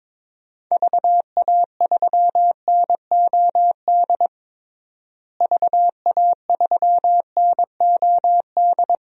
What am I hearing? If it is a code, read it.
Morse code, heard as a keyed tone, about 22 wpm, 700 Hz. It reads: VA3NOD VA3NOD
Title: DXCC-100 Call Signs - 6 Characters - Repeated 2x - 22wpm